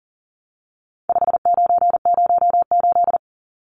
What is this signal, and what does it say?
Morse code, heard as a keyed tone, about 40 wpm, 700 Hz. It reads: 5908
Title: Numbers - 4-Digit 40wpm